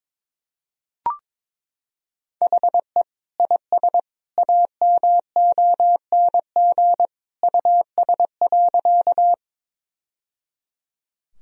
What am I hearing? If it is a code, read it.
Morse code, heard as a keyed tone, about 22 wpm, 700 Hz. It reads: HE IS AMONG US.
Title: Sentences from Top 1000 Words - Review Part 3 - 22wpm